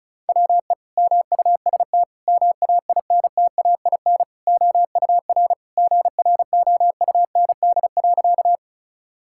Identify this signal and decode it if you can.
Morse code, heard as a keyed tone, about 36 wpm, 700 Hz. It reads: WE MUST MAINTAIN OUR GROUND.
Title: Sentences from Top 1000 Words - Review Part 5 - 35wpm